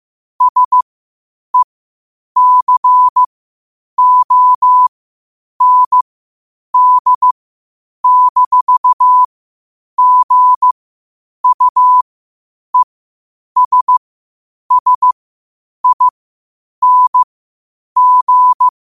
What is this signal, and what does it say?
Morse code, heard as a keyed tone, about 15 wpm, spaced as for 8 wpm, 1000 Hz. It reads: SECOND-GUESSING
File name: qotdfiles / 8wpm